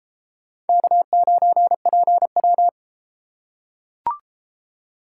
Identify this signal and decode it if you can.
Morse code, heard as a keyed tone, about 30 wpm, 700 Hz. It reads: K9PW